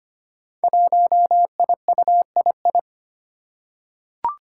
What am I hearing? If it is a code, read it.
Morse code, heard as a keyed tone, about 25 wpm, 700 Hz. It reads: 1IUII
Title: Five Challenging Letter-Number Combinations 25wpm